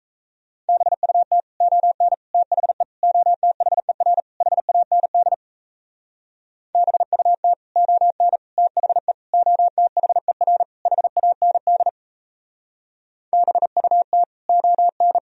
Morse code, heard as a keyed tone, about 37 wpm, 700 Hz. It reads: BUT ON THE OTHER HAND BUT ON THE OTHER HAND BUT ON